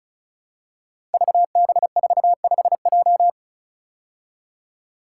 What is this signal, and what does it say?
Morse code, heard as a keyed tone, about 36 wpm, 700 Hz. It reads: VB45J
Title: Code Groups - 5 Letter:Numbers - 35wpm